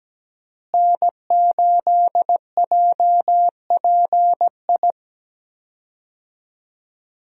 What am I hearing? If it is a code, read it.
Morse code, heard as a keyed tone, about 17 wpm, 700 Hz. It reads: N8JPI
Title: USA Call Signs - 5 Characters - Repeated 2x - 17wpm